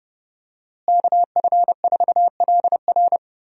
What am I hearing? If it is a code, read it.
Morse code, heard as a keyed tone, about 30 wpm, 700 Hz. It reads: KF4LR
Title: USA Call Signs - 5 Characters - Repeated 3x - 30wpm